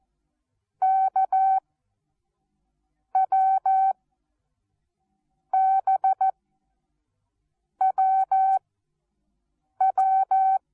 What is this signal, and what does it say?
Morse code, heard as a keyed tone, about 14 wpm, spaced as for 5 wpm, 770 Hz. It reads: KWBWW
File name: Onlinelehrgang / morsen / DJ4UF Morsen19